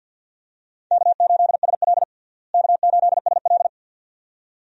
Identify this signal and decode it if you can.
Morse code, heard as a keyed tone, about 52 wpm, 700 Hz. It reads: K8SL K8SL
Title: USA Call Signs - 4 Characters - Repeated 2x - 50wpm